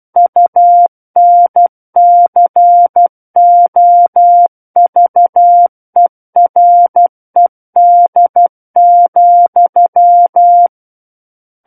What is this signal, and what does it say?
Morse code, heard as a keyed tone, about 12 wpm, 700 Hz. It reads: UNCOVERED,
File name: Book One / 03-Chapter 3 - 12 WPM